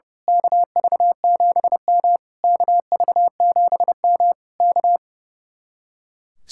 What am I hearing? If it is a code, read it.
Morse code, heard as a keyed tone, about 28 wpm, 700 Hz. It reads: KV7M KV7M K
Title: Speed-Racing Common QSO Phrases Encore - 30wpm